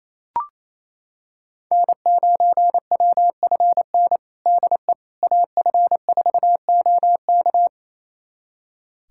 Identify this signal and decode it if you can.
Morse code, heard as a keyed tone, about 28 wpm, 700 Hz. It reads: N9WFN DE AF4OK
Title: USA Callsign DE Callsign - 5 Characters - 28wpm